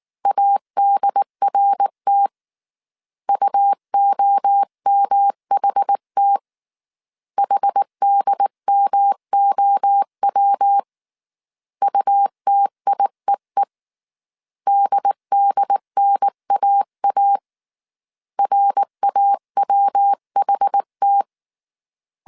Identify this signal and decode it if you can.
Morse code, heard as a keyed tone, about 19 wpm, 800 Hz. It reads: ADRT UOMHT HDMOW UTIEE DDNAA RAWHT